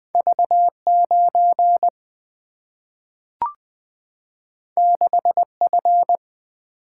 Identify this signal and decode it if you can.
Morse code, heard as a keyed tone, about 20 wpm, 700 Hz. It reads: V9 6F
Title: Single-Letter Single-Number 20wpm